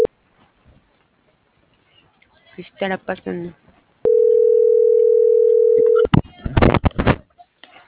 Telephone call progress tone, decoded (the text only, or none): ringback tone